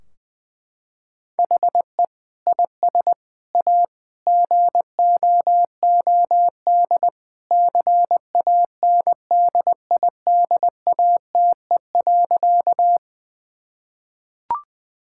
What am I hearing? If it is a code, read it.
Morse code, heard as a keyed tone, about 20 wpm, 700 Hz. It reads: HE IS A GOOD CANDIDATE.
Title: Sentences from Top 1600 Words 20wpm